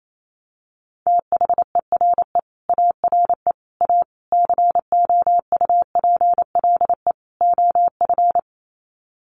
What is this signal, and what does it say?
Morse code, heard as a keyed tone, about 28 wpm, 700 Hz. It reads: THERE ARE A COUPLE OF
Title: Sets of 5 Words Encore 28wpm